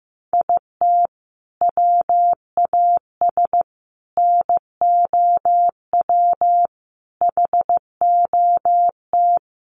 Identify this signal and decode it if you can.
Morse code, heard as a keyed tone, about 15 wpm, 700 Hz. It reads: IT WAS NOW HOT
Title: Sentences from Top 1200 Words 15wpm